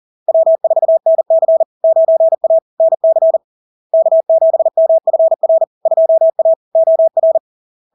Morse code, heard as a keyed tone, about 40 wpm, 650 Hz. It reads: W4NC 9A NC K7MFR 2A OR
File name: LARK-FD-2022-Log-in-CW-40wpm-650-Hz